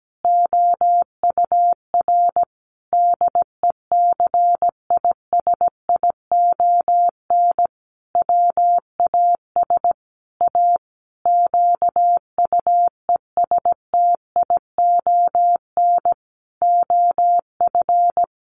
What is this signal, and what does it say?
Morse code, heard as a keyed tone, about 17 wpm, 700 Hz. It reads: OUR DECISION WAS A QUESTION OF
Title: Sentences from Top 1400 Words 17wpm